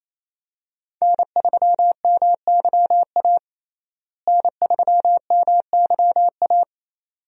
Morse code, heard as a keyed tone, about 28 wpm, 700 Hz. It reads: N3MYA N3MYA
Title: USA Call Signs - 5 Characters - Repeated 2x - 28wpm